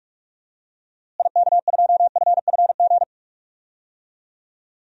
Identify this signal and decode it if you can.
Morse code, heard as a keyed tone, about 46 wpm, 700 Hz. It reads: IK2FFG
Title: DXCC-100 Call Signs - 6 Characters - Repeated 2x - 45wpm